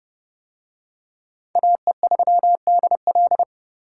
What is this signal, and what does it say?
Morse code, heard as a keyed tone, about 27 wpm, 700 Hz. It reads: AE3DL